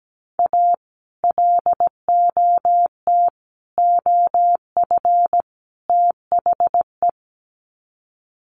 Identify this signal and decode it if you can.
Morse code, heard as a keyed tone, about 17 wpm, 700 Hz. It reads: A LOT OF THE
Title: Sets of 4 Words 17wpm